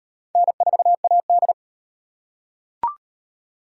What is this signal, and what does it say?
Morse code, heard as a keyed tone, about 33 wpm, 700 Hz. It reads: N4AD